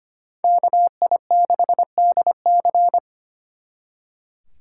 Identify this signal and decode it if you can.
Morse code, heard as a keyed tone, about 30 wpm, 700 Hz. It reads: KI6DC
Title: Speed Racing - USA 2x2 Call Signs 25wpm